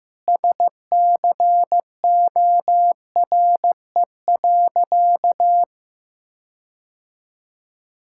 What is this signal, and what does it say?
Morse code, heard as a keyed tone, about 15 wpm, 700 Hz. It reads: SCORE.
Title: Sentences from Top 1600 Words 15wpm